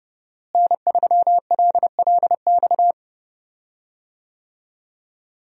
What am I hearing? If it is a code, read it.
Morse code, heard as a keyed tone, about 30 wpm, 700 Hz. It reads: N3LLX
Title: USA Call Signs - 5 Characters - Repeated 3x - 30wpm